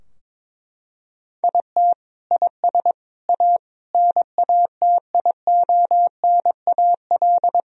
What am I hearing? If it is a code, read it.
Morse code, heard as a keyed tone, about 22 wpm, 700 Hz. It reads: IT IS A NATIONAL
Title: Sentences from Top 1000 Words - Review Part 4 - 22wpm